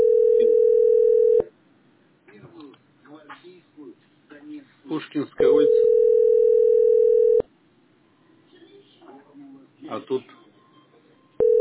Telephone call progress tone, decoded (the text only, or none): ringback tone